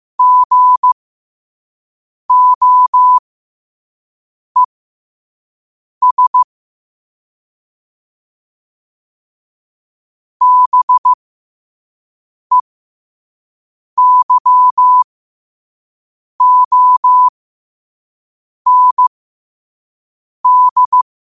Morse code, heard as a keyed tone, about 15 wpm, spaced as for 5 wpm, 1000 Hz. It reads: GOES BEYOND
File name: qotdfiles / 5wpm